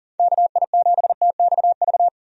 Morse code, heard as a keyed tone, about 38 wpm, 700 Hz. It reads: KI7TXV
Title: USA Call Signs - 6 Characters - 40wpm